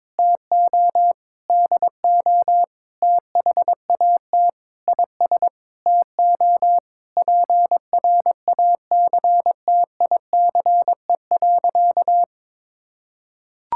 Morse code, heard as a keyed tone, about 22 wpm, 700 Hz. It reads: TO DO THAT IS TO PRACTICE.
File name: Sentences with 9-Word Phrases - 22wpm